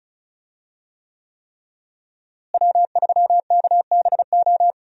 Morse code, heard as a keyed tone, about 36 wpm, 700 Hz. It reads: W3KBO
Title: USA Call Signs - 5 Characters - Repeated 3x - 35wpm